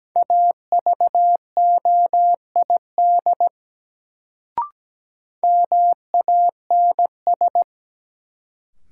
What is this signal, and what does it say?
Morse code, heard as a keyed tone, about 17 wpm, 700 Hz. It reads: AVOID MANS
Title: Sentences from Top 1000 Words - Review Part 6 - New Word Drill - 17wpm